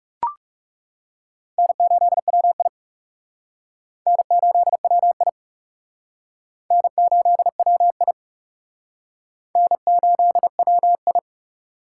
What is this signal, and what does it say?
Morse code, heard as a keyed tone, about 37 wpm, 700 Hz. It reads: N8WI N8WI N8WI N8WI
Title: Speed Racing - USA 1x2 Call Signs 30wpm